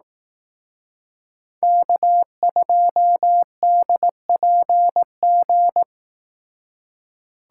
Morse code, heard as a keyed tone, about 18 wpm, 700 Hz. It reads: K2DPG
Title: Speed Racing - USA Call Signs 15wpm